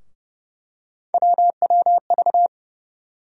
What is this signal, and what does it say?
Morse code, heard as a keyed tone, about 30 wpm, 700 Hz. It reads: WWV
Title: USA Call Sign Postfixes - 3 Letters 30wpm